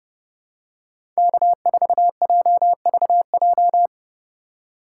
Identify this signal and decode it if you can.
Morse code, heard as a keyed tone, about 30 wpm, 690 Hz. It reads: K4JVJ
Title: USA Call Signs - 5 Characters - 2x Character Spacing - 30wpm